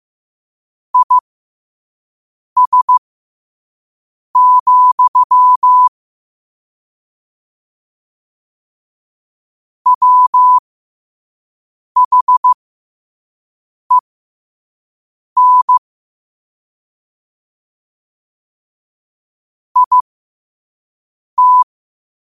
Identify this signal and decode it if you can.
Morse code, heard as a keyed tone, about 15 wpm, spaced as for 5 wpm, 1000 Hz. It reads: IS, WHEN IT
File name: qotdfiles / 5wpm